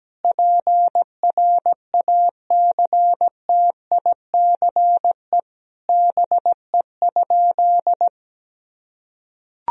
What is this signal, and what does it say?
Morse code, heard as a keyed tone, about 17 wpm, 700 Hz. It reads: PRACTICE BE?
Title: Sentences from Top 1000 Words - Review Part 4 - 17wpm